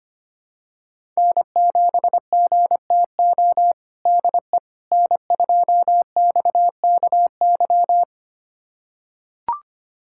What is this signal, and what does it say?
Morse code, heard as a keyed tone, about 25 wpm, 700 Hz. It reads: N7GTO DE N2XKY
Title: USA Callsign DE Callsign - 5 Characters - 25wpm